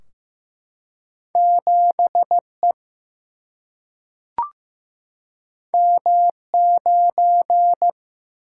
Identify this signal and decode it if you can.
Morse code, heard as a keyed tone, about 15 wpm, 700 Hz. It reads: 7E M9
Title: Single-Letter Single-Number 15wpm